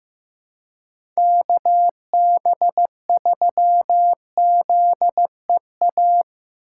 Morse code, heard as a keyed tone, about 16 wpm, 700 Hz. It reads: KB3ZEA
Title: USA Call Signs - 6 Characters - 2x Character Spacing - 15wpm